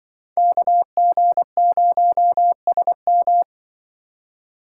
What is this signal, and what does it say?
Morse code, heard as a keyed tone, about 25 wpm, 700 Hz. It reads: KG0SM